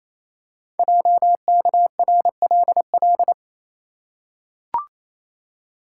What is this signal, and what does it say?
Morse code, heard as a keyed tone, about 28 wpm, 700 Hz. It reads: JKRLL
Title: Code Groups - 5 Letters - 28wpm